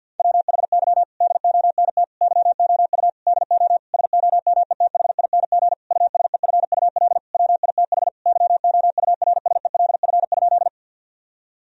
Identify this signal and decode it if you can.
Morse code, heard as a keyed tone, about 50 wpm, 700 Hz. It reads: WHY DONT YOU DO SOMETHING USEFUL WITH YOURSELF?
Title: Sentences from Top 1000 Words - Review Part 4 - 50wpm